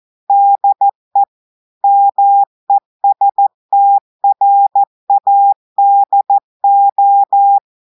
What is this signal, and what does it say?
Morse code, heard as a keyed tone, about 14 wpm, 800 Hz. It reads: DE MESTRADO
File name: audios / portugais / portugais texte en clair 1 vitesse 14 mots minutes